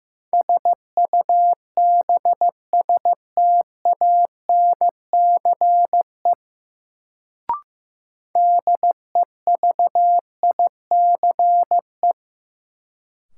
Morse code, heard as a keyed tone, about 15 wpm, 700 Hz. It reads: SUBSTANCE DEVICE
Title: Sentences from Top 1500 Words - New Word Drill - 15wpm